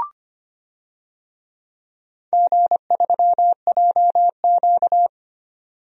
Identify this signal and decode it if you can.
Morse code, heard as a keyed tone, about 25 wpm, 700 Hz. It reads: G3JQ
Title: DXCC-100 Call Signs - 4 Characters - 25wpm